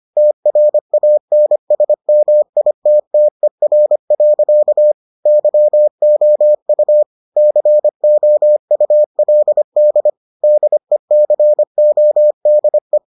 Morse code, heard as a keyed tone, about 25 wpm, 600 Hz. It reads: TRANSMITTER. YOU COULD DECODE